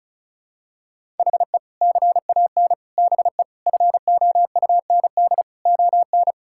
Morse code, heard as a keyed tone, about 34 wpm, 700 Hz. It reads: HE CAN BE FOUND ON
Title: Sentences from Top 1600 Words 35wpm